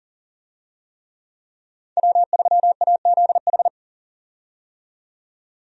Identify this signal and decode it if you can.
Morse code, heard as a keyed tone, about 40 wpm, 700 Hz. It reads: W3AZH
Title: USA Call Signs - 5 Characters - 40wpm